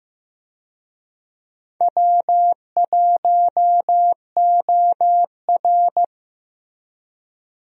Morse code, heard as a keyed tone, about 15 wpm, 700 Hz. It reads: W1OR
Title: USA Call Signs - 4 Characters - Repeated 3x - 15wpm